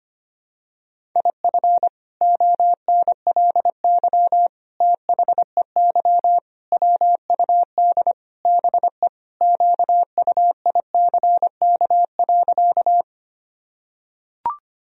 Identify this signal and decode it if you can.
Morse code, heard as a keyed tone, about 25 wpm, 700 Hz. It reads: IF ONLY THEY WUD BE QUICK.
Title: Sentences from Top 500 Words and Abbreviations 25wpm